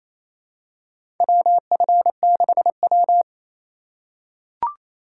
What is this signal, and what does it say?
Morse code, heard as a keyed tone, about 28 wpm, 700 Hz. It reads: WF6W
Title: USA Call Signs - 4 Characters - 2x Character Spacing - 28wpm